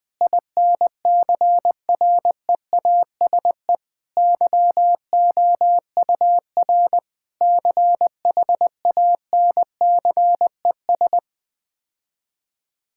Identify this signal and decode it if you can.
Morse code, heard as a keyed tone, about 21 wpm, 700 Hz. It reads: INCREASE YOUR CHANCES
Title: Sets of 8 Words - 20wpm